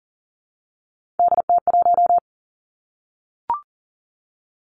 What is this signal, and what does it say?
Morse code, heard as a keyed tone, about 42 wpm, 690 Hz. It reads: DT1
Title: Three Challenging Letter-Number Combinations 40wpm